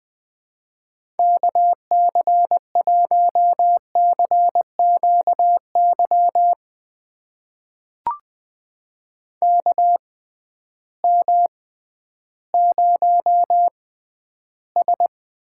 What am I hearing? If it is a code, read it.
Morse code, heard as a keyed tone, about 21 wpm, spaced as for 12 wpm, 700 Hz. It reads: KC1CQY KM0S
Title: USA Call Signs - 6 Characters - 6x Character Spacing - 20wpm